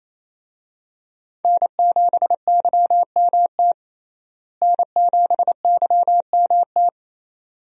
Morse code, heard as a keyed tone, about 28 wpm, 700 Hz. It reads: N7YMT N7YMT
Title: USA Call Signs - 5 Characters - Repeated 2x - 28wpm